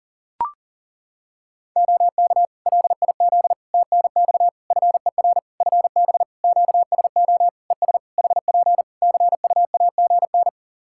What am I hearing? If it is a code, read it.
Morse code, heard as a keyed tone, about 37 wpm, 700 Hz. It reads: OK LIZ TNX FER FB QSO ES HP CUAGN
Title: Practice QSO - Exchange 4 - Ending - 40wpm